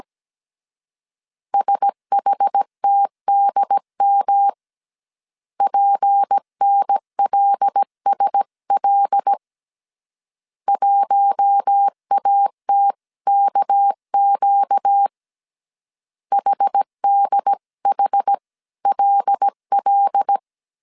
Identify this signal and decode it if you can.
Morse code, heard as a keyed tone, about 17 wpm, 800 Hz. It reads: SHTDM PNLSL 1ATKQ HDHLL
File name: CW / DET / 1 6 1500C S A 16wpm 2635